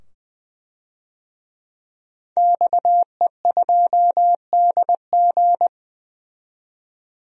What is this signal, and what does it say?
Morse code, heard as a keyed tone, about 21 wpm, 700 Hz. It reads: XE2DG
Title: DXCC-100 Call Signs - 5 Characters - 20wpm